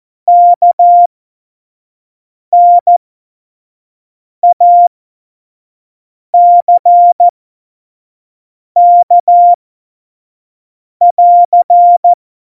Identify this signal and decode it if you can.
Morse code, heard as a keyed tone, about 14 wpm, spaced as for 5 wpm, 700 Hz. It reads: KNACK <AR>
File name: Morsekurs / Witze / Witz 13. November 2018, 25 F70